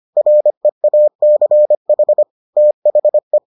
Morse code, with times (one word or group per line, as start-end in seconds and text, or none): 0.2-2.2 REACH
2.6-3.4 THE